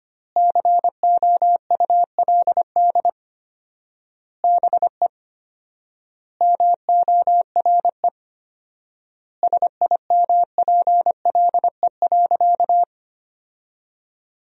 Morse code, heard as a keyed tone, about 25 wpm, 700 Hz. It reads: COULD BE MORE SIMPLE.
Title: 4x Word-Spacing - Sentences from Top 500 Words - 25wpm